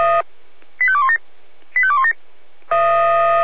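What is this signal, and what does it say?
Five-tone selective call (ZVEI), heard as quick runs of five tones, tones of about 70 ms. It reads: 85217 85217
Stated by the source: ZVEI five-tone selective call 85248 received on SDR.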